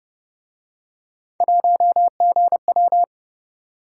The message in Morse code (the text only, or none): 1GW